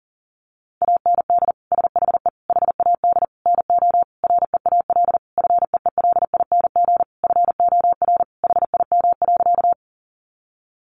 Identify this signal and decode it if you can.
Morse code, heard as a keyed tone, about 40 wpm, 700 Hz. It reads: AND SHE HAD NO REAL FEELING FOR HIM.
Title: Sentences from Top 1200 Words 40wpm